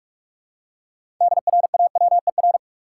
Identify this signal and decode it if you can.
Morse code, heard as a keyed tone, about 46 wpm, 700 Hz. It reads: DRAWER
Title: Six Letter Words - 45wpm